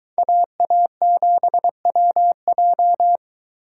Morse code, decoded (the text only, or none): AA7WJ